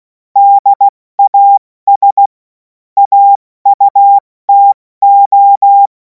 Morse code, heard as a keyed tone, about 16 wpm, 800 Hz. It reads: DAS AUTO
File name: Morsekurs / Witze / Witz 28. April 2026 70 F80